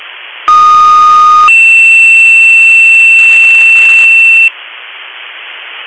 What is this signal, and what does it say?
Two-tone page, heard as a one-second tone, then a three-second tone, about 1230 then 2700 Hz. It is heard from 0.5 to 4.5 s.